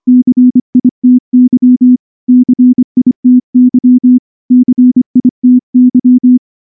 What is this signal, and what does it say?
Morse code, heard as a keyed tone, about 24 wpm, 250 Hz. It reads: CITY CITY CITY
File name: morse-training-25WPM